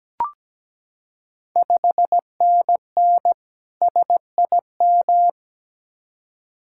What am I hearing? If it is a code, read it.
Morse code, heard as a keyed tone, about 17 wpm, 690 Hz. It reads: 5NN SIM
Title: Ontario QSO Party Exchange 17wpm